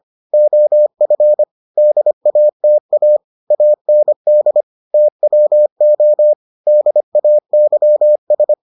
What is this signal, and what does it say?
Morse code, heard as a keyed tone, about 25 wpm, 600 Hz. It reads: OF DATA AND TWO DAYS